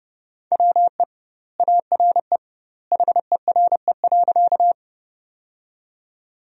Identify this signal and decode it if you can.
Morse code, heard as a keyed tone, about 32 wpm, 700 Hz. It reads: WE ARE HERE.
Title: 2x Word-Spacing - Sentences from Top 200 Words - 30wpm